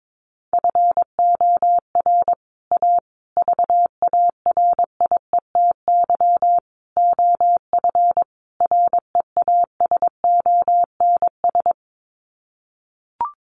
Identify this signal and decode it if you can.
Morse code, heard as a keyed tone, about 22 wpm, 700 Hz. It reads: FOR A VARIETY OF REASONS
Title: Sets of 5 Words Encore 22wpm